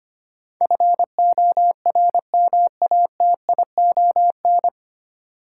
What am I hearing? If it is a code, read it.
Morse code, heard as a keyed tone, about 25 wpm, 700 Hz. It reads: FORMATION
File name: practice / 25wpm / 9 Letter Words - First 3 Letters Given - 25wpm